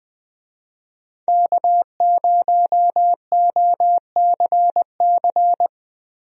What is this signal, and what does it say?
Morse code, heard as a keyed tone, about 20 wpm, 700 Hz. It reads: K0OCC
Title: USA Call Signs - 5 Characters - Repeated 2x - 20wpm